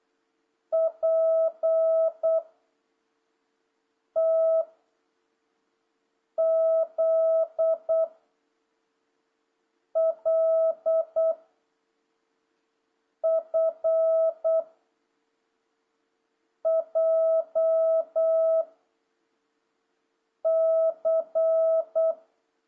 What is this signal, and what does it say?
Morse code, heard as a keyed tone, about 8 wpm, 650 Hz. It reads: P T Z L F J C
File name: jh8cbh / 2026cw